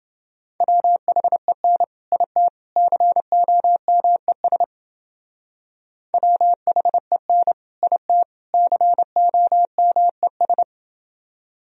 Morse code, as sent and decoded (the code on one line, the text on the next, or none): .-- .... -. / .. - / -.-. --- -- . ... / .-- .... -. / .. - / -.-. --- -- . ...
WHEN IT COMES WHEN IT COMES